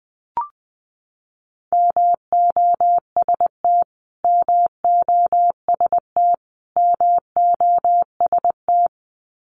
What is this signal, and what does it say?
Morse code, heard as a keyed tone, about 20 wpm, 700 Hz. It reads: MOST MOST MOST
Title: Top 500 Words Repeated 20wpm